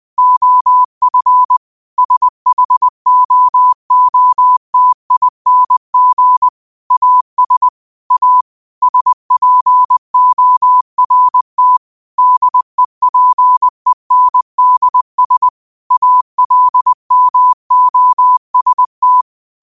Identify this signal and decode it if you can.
Morse code, heard as a keyed tone, about 20 wpm, 1000 Hz. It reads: OF SHOOTING AS A SPORT DEPENDS ALMOST
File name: qotdfiles / 20wpm